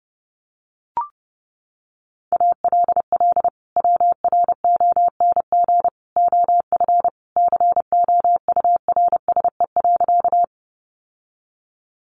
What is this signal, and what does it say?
Morse code, heard as a keyed tone, about 28 wpm, 700 Hz. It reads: ALL WRONG OF COURSE.
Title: Sentences from Top 900 Words 30wpm